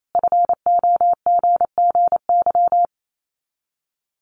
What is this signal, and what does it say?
Morse code, heard as a keyed tone, about 28 wpm, 700 Hz. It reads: FOGGY
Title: Common QSO Phrases 28wpm